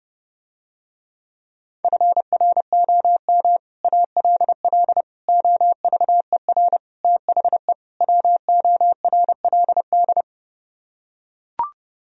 Morse code, heard as a keyed tone, about 30 wpm, 700 Hz. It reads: FROM ALL OVER THE WORLD